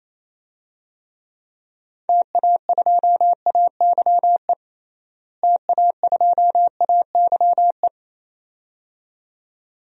Morse code, heard as a keyed tone, about 28 wpm, 700 Hz. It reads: TA2AYE TA2AYE
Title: DXCC-100 Call Signs - 6 Characters - Repeated 2x - 28wpm